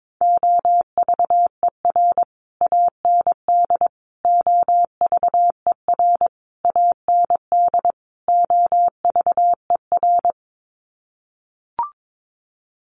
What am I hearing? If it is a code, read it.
Morse code, heard as a keyed tone, about 22 wpm, 700 Hz. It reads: OVER AND OVER AND OVER
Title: Sets of 5 Words Encore 22wpm